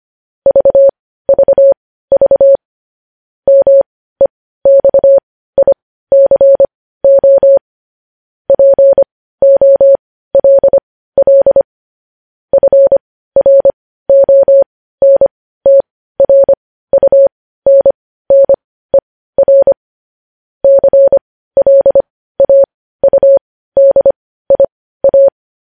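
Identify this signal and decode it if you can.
Morse code, heard as a keyed tone, about 25 wpm, spaced as for 15 wpm, 550 Hz. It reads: VVV MEXICO POLL FRONTRUNNER CLAUDIA
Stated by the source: Morse code news headlines sent at 15.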